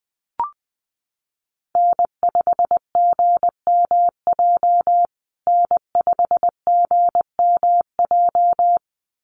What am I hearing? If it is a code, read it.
Morse code, heard as a keyed tone, about 20 wpm, 700 Hz. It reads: N5GMJ N5GMJ
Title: USA Call Signs Repeated 2x 20wpm